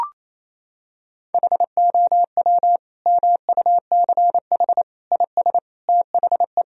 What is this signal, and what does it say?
Morse code, heard as a keyed tone, about 28 wpm, 700 Hz. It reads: HOW MUCH IS THE
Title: Sentences from Top 1000 Words - Review Part 3 - 28wpm